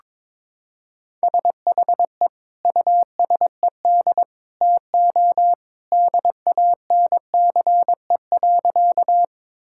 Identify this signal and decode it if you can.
Morse code, heard as a keyed tone, about 22 wpm, 700 Hz. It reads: SHE USED TO DANCE.
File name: practice / 22wpm / Sentences with 2-Word Phrases - 22wpm